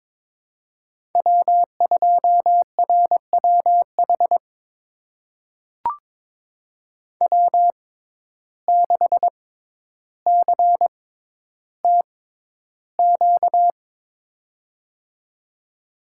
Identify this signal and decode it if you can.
Morse code, heard as a keyed tone, about 22 wpm, spaced as for 11 wpm, 700 Hz. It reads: W2RWH W6CTQ
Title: USA Call Signs - 5 Characters - 6x Character Spacing - 22wpm